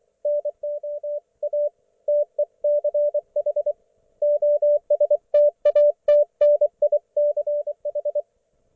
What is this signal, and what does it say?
Morse code, heard as a keyed tone, about 24 wpm, 590 Hz. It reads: NO A TECH OSTATNICH
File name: Archiv / CUC212 / ok1fhi cul212